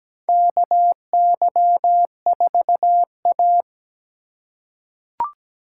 Morse code, heard as a keyed tone, about 17 wpm, 700 Hz. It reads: KY4A
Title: USA Call Signs - 4 Characters - 3x Character Spacing - 17wpm